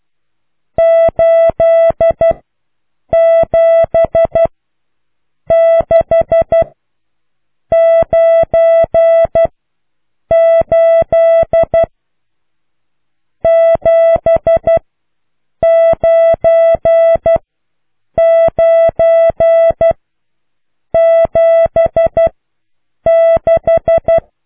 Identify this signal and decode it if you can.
Morse code, heard as a keyed tone, about 12 wpm, spaced as for 7 wpm, 660 Hz. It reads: 87698 79976